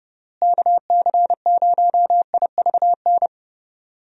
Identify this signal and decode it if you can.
Morse code, heard as a keyed tone, about 31 wpm, 700 Hz. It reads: KC0IVN
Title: USA Call Signs - 6 Characters - Repeated 2x - 30wpm